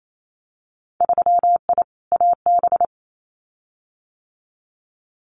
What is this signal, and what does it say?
Morse code, heard as a keyed tone, about 28 wpm, 700 Hz. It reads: 3I AB
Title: Winter Field Day Exchanges Encore 28wpm